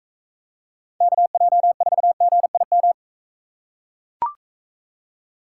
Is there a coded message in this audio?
Yes: Morse code from 1.0 to 2.9 s, heard as a keyed tone, about 44 wpm, 700 Hz. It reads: KJ4GIM